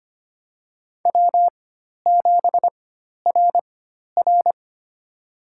Morse code, heard as a keyed tone, about 25 wpm, spaced as for 12 wpm, 700 Hz. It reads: W7RR
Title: USA Call Signs - 4 Characters - 4x Character Spacing - 25wpm